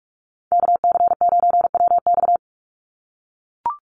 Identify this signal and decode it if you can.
Morse code, heard as a keyed tone, about 44 wpm, 700 Hz. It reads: KC9WX
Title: USA Call Signs 45wpm